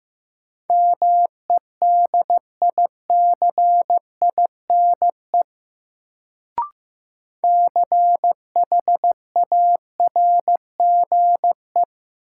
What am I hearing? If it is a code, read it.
Morse code, heard as a keyed tone, about 15 wpm, 700 Hz. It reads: MEDICINE CHARGE
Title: Sentences from Top 900 Words - New Word Drill - 15wpm